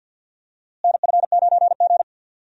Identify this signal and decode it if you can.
Morse code, heard as a keyed tone, about 52 wpm, 700 Hz. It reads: NF9G
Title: USA Call Signs - 4 Characters - Repeated 3x - 50wpm